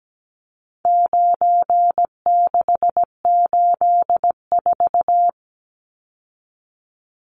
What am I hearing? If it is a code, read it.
Morse code, heard as a keyed tone, about 17 wpm, 700 Hz. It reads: 9684
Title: Numbers - 4-Digit 17wpm